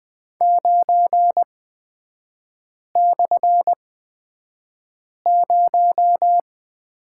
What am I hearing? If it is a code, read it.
Morse code, heard as a keyed tone, about 20 wpm, spaced as for 5 wpm, 700 Hz. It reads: 9/0
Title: 224 Lesson 36e - BT - Three Characters 20wpm